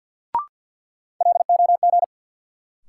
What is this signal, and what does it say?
Morse code, heard as a keyed tone, about 53 wpm, 700 Hz. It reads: LOG
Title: Sentences from Top 1400 Words - New Word Drill Alt - 50wpm